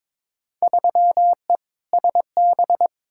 Morse code, heard as a keyed tone, about 22 wpm, 700 Hz. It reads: 3E SB
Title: Field Day Exchanges Encore - 22wpm